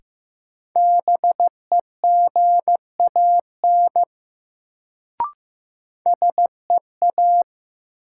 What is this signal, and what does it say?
Morse code, heard as a keyed tone, about 15 wpm, 700 Hz. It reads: BEGAN SEA
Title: Sentences from Top 300 Words - New Word Drill - 15wpm